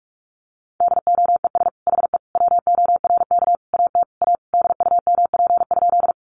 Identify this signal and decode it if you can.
Morse code, heard as a keyed tone, about 44 wpm, 700 Hz. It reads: DOES HE WORK AT A DUMP?
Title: Sentences from Top 1300 Words 45wpm